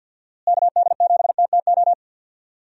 Morse code, heard as a keyed tone, about 50 wpm, 700 Hz. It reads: KD7TTO
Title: USA Call Signs - 6 Characters - Repeated 3x - 50wpm